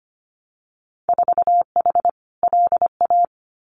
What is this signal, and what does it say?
Morse code, heard as a keyed tone, about 25 wpm, 700 Hz. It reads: 4H LA
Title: Winter Field Day Exchanges Encore 25wpm